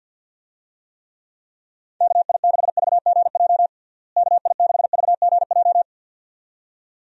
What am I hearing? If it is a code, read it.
Morse code, heard as a keyed tone, about 51 wpm, 700 Hz. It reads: KI6VGJ KI6VGJ
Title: USA Call Signs - 6 Characters - Repeated 2x - 50wpm